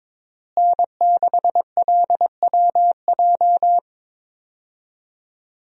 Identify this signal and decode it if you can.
Morse code, heard as a keyed tone, about 22 wpm, 700 Hz. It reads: N6LWJ
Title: USA Call Signs - 5 Characters - Repeated 3x - 22wpm